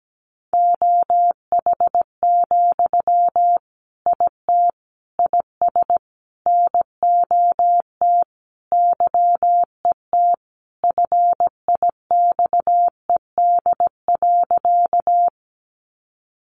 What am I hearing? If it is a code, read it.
Morse code, heard as a keyed tone, about 17 wpm, 700 Hz. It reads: OH, IT IS NOT YET FIXED.
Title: Sentences from Top 1100 Words 17wpm